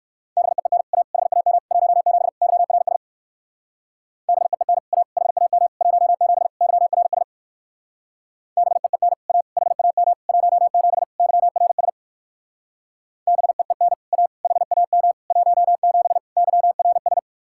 Morse code, heard as a keyed tone, about 57 wpm, 700 Hz. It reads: BEEN A HAM 17 YRS BEEN A HAM 17 YRS BEEN A HAM 17 YRS BEEN A HAM 17 YRS